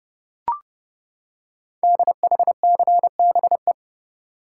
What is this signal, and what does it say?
Morse code, heard as a keyed tone, about 30 wpm, 700 Hz. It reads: DHCBE
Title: Code Groups - 5 Letters - 30wpm